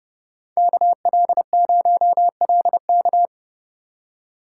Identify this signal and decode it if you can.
Morse code, heard as a keyed tone, about 28 wpm, 700 Hz. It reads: KL0LK